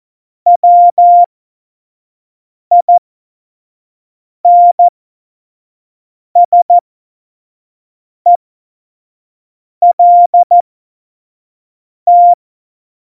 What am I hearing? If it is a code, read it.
Morse code, heard as a keyed tone, about 14 wpm, spaced as for 5 wpm, 700 Hz. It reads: WINSELT